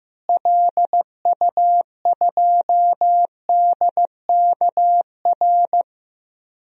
Morse code, heard as a keyed tone, about 15 wpm, 700 Hz. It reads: LU2DKR
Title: DXCC-100 Call Signs - 6 Characters - Repeated 2x - 15wpm